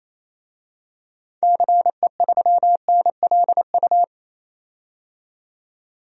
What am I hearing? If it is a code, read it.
Morse code, heard as a keyed tone, about 28 wpm, 700 Hz. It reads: CE3NLU
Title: DXCC-100 Call Signs - 6 Characters - 28wpm